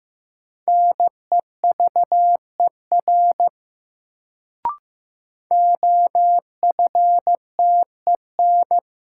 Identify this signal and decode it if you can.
Morse code, heard as a keyed tone, about 16 wpm, 700 Hz. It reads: NEVER OFTEN
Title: Sentences from Top 300 Words - New Word Drill - 15wpm